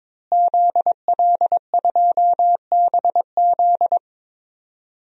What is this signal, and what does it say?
Morse code, heard as a keyed tone, about 22 wpm, 700 Hz. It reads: ZL2BZ
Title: DXCC-100 Call Signs - 5 Characters - Repeated 3x - 22wpm